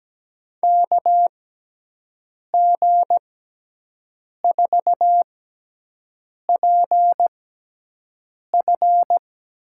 Morse code, heard as a keyed tone, about 17 wpm, spaced as for 6 wpm, 700 Hz. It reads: KG4PF
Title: USA Call Signs - 6 Characters - 6x Character Spacing - 17wpm